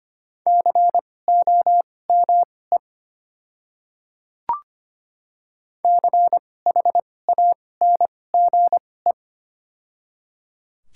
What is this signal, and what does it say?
Morse code, heard as a keyed tone, about 25 wpm, spaced as for 18 wpm, 700 Hz. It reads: COME CHANGE
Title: 2x Character Spacing - Top 200 Words - 25wpm